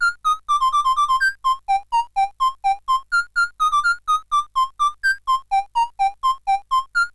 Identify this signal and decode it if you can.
There is no code.